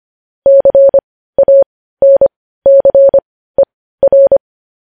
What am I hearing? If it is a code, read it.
Morse code, heard as a keyed tone, about 25 wpm, spaced as for 15 wpm, 550 Hz. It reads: CANCER
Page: news-headlines-in-morse-code-at-15-wpm